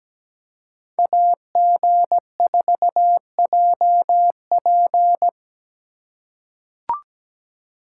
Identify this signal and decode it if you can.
Morse code, heard as a keyed tone, about 17 wpm, 700 Hz. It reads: AG4JP
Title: USA Call Signs - 5 Characters - 2x Character Spacing - 17wpm